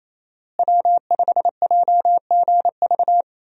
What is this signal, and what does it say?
Morse code, heard as a keyed tone, about 27 wpm, 700 Hz. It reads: W5JGV
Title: USA Call Signs - 5 Characters - Repeated 2x - 28wpm